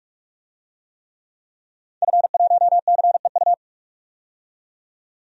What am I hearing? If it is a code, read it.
Morse code, heard as a keyed tone, about 46 wpm, 700 Hz. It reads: F1CEU
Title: DXCC-100 Call Signs - 5 Characters - 45wpm